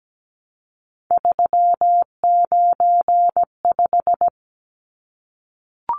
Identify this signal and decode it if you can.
Morse code, heard as a keyed tone, about 17 wpm, 700 Hz. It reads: 395
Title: Numbers - 3-Digit 17wpm